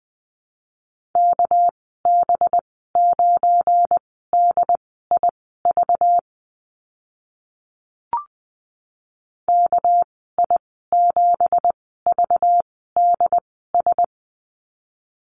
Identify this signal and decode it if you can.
Morse code, heard as a keyed tone, about 20 wpm, spaced as for 15 wpm, 700 Hz. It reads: KB9DIV KI7VDS
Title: 245 Lesson 39g - Letter z - Call Signs 20wpm